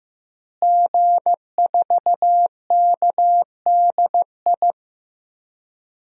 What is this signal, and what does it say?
Morse code, heard as a keyed tone, about 15 wpm, 700 Hz. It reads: G4KDI
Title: DXCC-100 Call Signs - 5 Characters - Repeated 3x - 15wpm